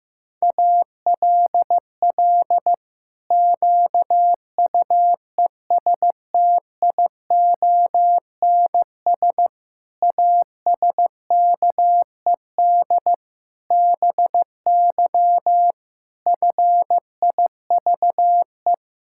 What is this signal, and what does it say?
Morse code, heard as a keyed tone, about 15 wpm, 700 Hz. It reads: ALL QUESTIONS ASKED BY FIVE
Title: Pangram Challenge v2 - 15wpm